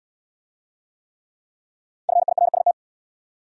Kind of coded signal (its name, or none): Morse code